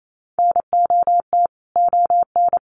Morse code, heard as a keyed tone, about 28 wpm, 700 Hz. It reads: NOT ON
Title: Sentences from Top 1200 Words 28wpm